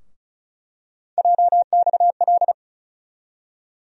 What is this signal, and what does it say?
Morse code, heard as a keyed tone, about 36 wpm, 700 Hz. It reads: JXL
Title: USA Call Sign Postfixes - 3 Letters 35wpm